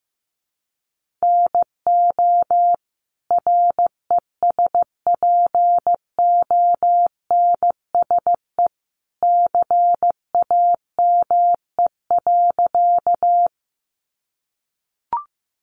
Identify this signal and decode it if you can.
Morse code, heard as a keyed tone, about 15 wpm, 700 Hz. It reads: NO RESPONSE CAME.
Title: Sentences from Top 1400 Words 15wpm